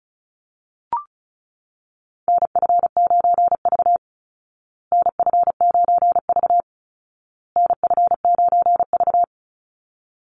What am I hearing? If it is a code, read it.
Morse code, heard as a keyed tone, about 34 wpm, 700 Hz. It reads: NF9V NF9V NF9V
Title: Call Signs Repeated 2x - 35wpm